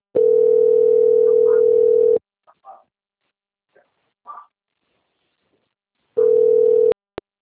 Ringback tone (North America), 440 + 480 Hz.
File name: out- - - - - .5600